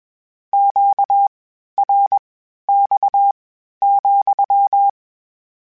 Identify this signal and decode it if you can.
Morse code, heard as a keyed tone, about 21 wpm, spaced as for 12 wpm, 800 Hz. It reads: QRX,
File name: Q Calls / Q Calls-mp3s / Q Calls sppc @20